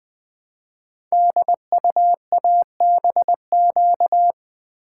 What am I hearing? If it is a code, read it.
Morse code, heard as a keyed tone, about 20 wpm, 700 Hz. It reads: DUABQ
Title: Code Groups - 5 Letters - 20wpm